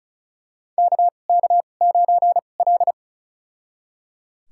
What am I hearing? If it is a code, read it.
Morse code, heard as a keyed tone, about 36 wpm, spaced as for 25 wpm, 700 Hz. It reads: KK9L
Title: USA Call Signs - 4 Characters - 2x Character Spacing - 35wpm